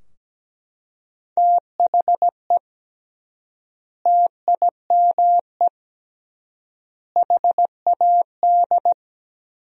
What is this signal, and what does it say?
Morse code, heard as a keyed tone, about 17 wpm, 700 Hz. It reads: THE TIME HAD
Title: 3x Word-Spacing - Sentences from Top 100 Words - 17wpm